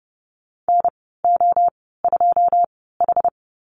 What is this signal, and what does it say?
Morse code, heard as a keyed tone, about 30 wpm, spaced as for 17 wpm, 700 Hz. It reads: NO2H
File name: Lesson 14g - Number 2 - Call Signs 30wpm